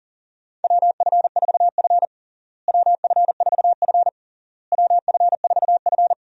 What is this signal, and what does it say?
Morse code, heard as a keyed tone, about 40 wpm, 700 Hz. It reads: WF4F WF4F WF4F
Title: USA Call Signs - 4 Characters - Repeated 3x - 40wpm